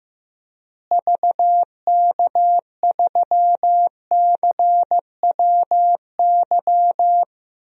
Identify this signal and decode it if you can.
Morse code, heard as a keyed tone, about 15 wpm, 700 Hz. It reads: VK3CWY
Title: DXCC-100 Call Signs - 6 Characters - Repeated 2x - 15wpm